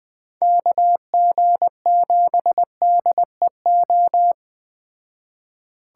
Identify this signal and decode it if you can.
Morse code, heard as a keyed tone, about 20 wpm, 700 Hz. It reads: KG7DEO
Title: Speed Racing - USA 2x3 Call Signs 17wpm